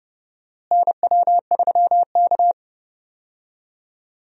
Morse code, heard as a keyed tone, about 31 wpm, 700 Hz. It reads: NW3K
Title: USA Call Signs - 4 Characters - Repeated 2x - 30wpm